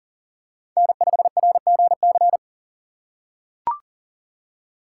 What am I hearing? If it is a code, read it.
Morse code, heard as a keyed tone, about 40 wpm, 700 Hz. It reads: N5RGC
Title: USA Call Signs - 5 Characters - 3x Character Spacing - 40wpm